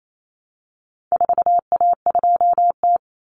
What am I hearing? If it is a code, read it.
Morse code, heard as a keyed tone, about 28 wpm, 700 Hz. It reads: 4A2T
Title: Common Cut Numbers - 4-Digit 28wpm